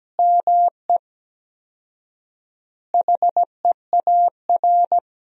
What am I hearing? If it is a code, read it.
Morse code, heard as a keyed tone, about 17 wpm, 700 Hz. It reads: ME HEAR
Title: 4x Word-Spacing - Sentences from Top 400 Words - 17wpm